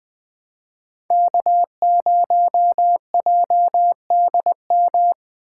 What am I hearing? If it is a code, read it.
Morse code, heard as a keyed tone, about 20 wpm, 700 Hz. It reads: K0JDM
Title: Speed Racing - USA 1x3 Call Signs 17wpm